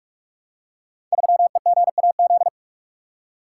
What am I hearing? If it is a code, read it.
Morse code, heard as a keyed tone, about 46 wpm, 700 Hz. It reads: 3EGAZ
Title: Code Groups - 5 Letter:Numbers - 45wpm